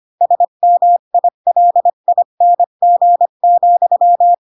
Morse code, heard as a keyed tone, about 25 wpm, 700 Hz. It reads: SMILING,
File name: Pinocchio / Pinocchio ch5 20 25 ik1hge